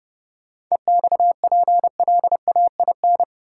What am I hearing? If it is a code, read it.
Morse code, heard as a keyed tone, about 30 wpm, 700 Hz. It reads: EXPLAIN
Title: Sentences from Top 1100 Words - New Word Drill - 30wpm